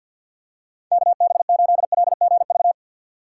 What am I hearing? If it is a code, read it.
Morse code, heard as a keyed tone, about 48 wpm, 700 Hz. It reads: KB8LGV
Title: USA Call Signs - 6x Character Spacing - 50wpm